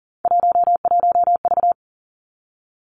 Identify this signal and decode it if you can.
Morse code, heard as a keyed tone, about 38 wpm, 700 Hz. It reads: 11V
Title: Three Challenging Letter-Number Combinations 40wpm